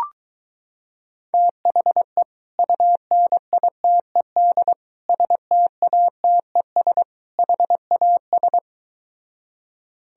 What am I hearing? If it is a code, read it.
Morse code, heard as a keyed tone, about 23 wpm, 700 Hz. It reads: THE UNITED STATES HAS